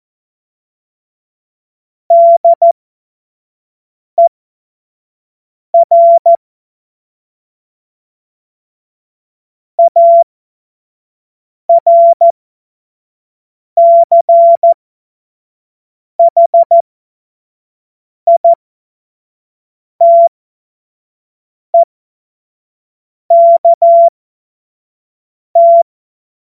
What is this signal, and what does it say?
Morse code, heard as a keyed tone, about 14 wpm, spaced as for 5 wpm, 680 Hz. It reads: DER ARCHITEKT
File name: Morsekurs / Witze / Witz 14.November 2023 25 F70